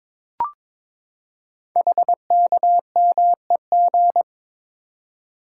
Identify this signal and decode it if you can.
Morse code, heard as a keyed tone, about 22 wpm, 700 Hz. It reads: HKMEG
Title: Code Groups - 5 Letters - 22wpm